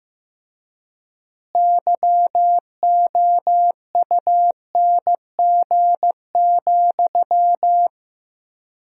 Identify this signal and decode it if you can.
Morse code, heard as a keyed tone, about 15 wpm, 700 Hz. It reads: YOUNG,
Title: 4x Word-Spacing - Sentences from Top 400 Words - 15wpm